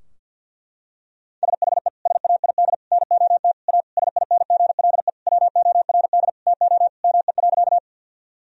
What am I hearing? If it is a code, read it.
Morse code, heard as a keyed tone, about 50 wpm, 700 Hz. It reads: SHE SAID NOT A SINGLE WORD TO ME.
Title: Sentences from Top 1000 Words - Review Part 4 - 50wpm